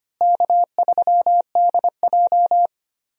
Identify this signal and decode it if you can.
Morse code, heard as a keyed tone, about 25 wpm, 700 Hz. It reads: K3DJ
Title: USA Call Signs - 4 Characters - Repeated 2x - 25wpm